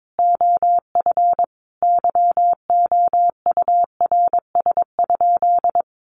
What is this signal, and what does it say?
Morse code, heard as a keyed tone, about 22 wpm, 700 Hz. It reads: OF YOURS?
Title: Sentences from Top 1500 Words 22wpm